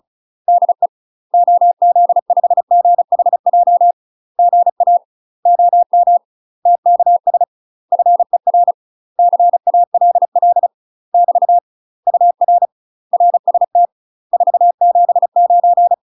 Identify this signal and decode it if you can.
Morse code, heard as a keyed tone, about 35 wpm, 700 Hz. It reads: DE OZ5GHJ GA OM TKS FER CALL <BT> UR RST 479
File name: UK MASTER QSOs TRAINING/35WPM